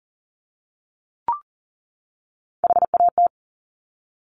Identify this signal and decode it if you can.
Morse code, heard as a keyed tone, about 42 wpm, 700 Hz. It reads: HAT
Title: Top Three Letter Combinations in Words 40wpm